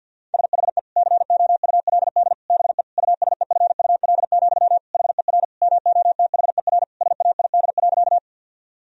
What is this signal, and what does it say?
Morse code, heard as a keyed tone, about 50 wpm, 700 Hz. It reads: SHE COULD BE USEFUL, HER MOTHER SAID.
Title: Sentences from Top 1000 Words - Review Part 4 - 50wpm